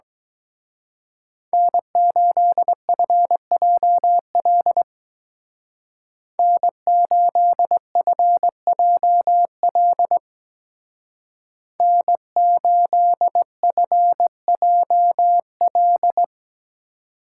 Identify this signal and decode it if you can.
Morse code, heard as a keyed tone, about 21 wpm, 700 Hz. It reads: N8FJL N8FJL N8FJL
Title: Speed Racing - USA 1x3 Call Signs 17wpm